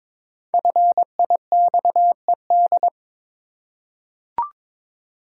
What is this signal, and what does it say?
Morse code, heard as a keyed tone, about 22 wpm, 690 Hz. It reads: FIXED
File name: 5 Letter Words - First 3 Letters Given - 22wpm